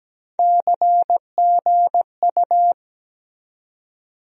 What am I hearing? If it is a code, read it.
Morse code, heard as a keyed tone, about 17 wpm, 700 Hz. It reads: CGU
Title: USA Call Sign Postfixes - 3 Letters 17wpm